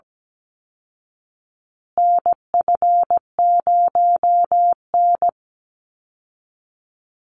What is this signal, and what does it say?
Morse code, heard as a keyed tone, about 17 wpm, 700 Hz. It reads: NF0N
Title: Call Signs Repeated 3x - 17wpm